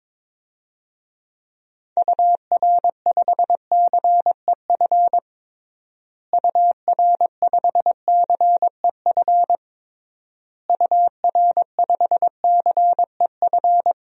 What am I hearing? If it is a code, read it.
Morse code, heard as a keyed tone, about 22 wpm, 700 Hz. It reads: UR5CEF UR5CEF UR5CEF
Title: DXCC-100 Call Signs - 6 Characters - Repeated 3x - 22wpm